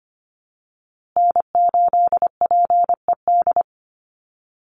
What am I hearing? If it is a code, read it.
Morse code, heard as a keyed tone, about 25 wpm, 700 Hz. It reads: N8PED
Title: USA Call Signs Repeated 2x 25wpm